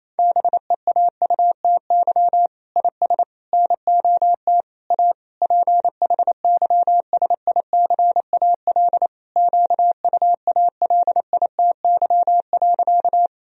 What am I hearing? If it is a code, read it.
Morse code, heard as a keyed tone, about 28 wpm, 700 Hz. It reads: BEAUTY IS NOT A PHYSICAL QUALITY.
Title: Sentences from Top 1000 Words - Review Part 4 - 28wpm